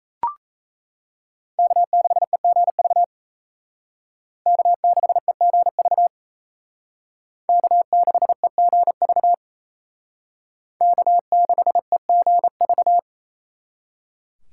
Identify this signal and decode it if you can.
Morse code, heard as a keyed tone, about 35 wpm, 700 Hz. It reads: K6EGV K6EGV K6EGV K6EGV